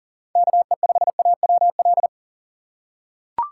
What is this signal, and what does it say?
Morse code, heard as a keyed tone, about 40 wpm, 700 Hz. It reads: KE5AWL
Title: USA Call Signs - 6 Characters - 6x Character Spacing - 40wpm